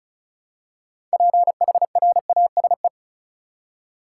Morse code, heard as a keyed tone, about 36 wpm, 700 Hz. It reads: PHRASE
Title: Six Letter Words - 35wpm